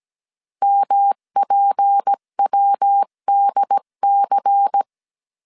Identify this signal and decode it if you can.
Morse code, heard as a keyed tone, about 17 wpm, 800 Hz. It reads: MPWDC
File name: CW / DET / 1 6 1500C S C 16wpm 2733